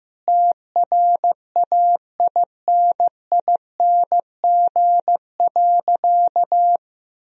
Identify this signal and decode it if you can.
Morse code, heard as a keyed tone, about 15 wpm, 700 Hz. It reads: TRAINING.
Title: Sentences from Top 1000 Words - Review Part 4 - 15wpm